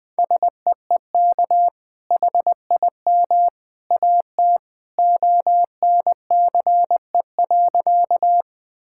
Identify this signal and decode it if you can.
Morse code, heard as a keyed tone, about 20 wpm, 700 Hz. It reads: SEEK HIM AT ONCE.
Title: Sentences from Top 1600 Words 20wpm